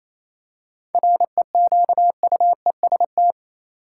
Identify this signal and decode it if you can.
Morse code, heard as a keyed tone, about 28 wpm, 700 Hz. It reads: REQUEST
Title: Seven Letter Words - 28wpm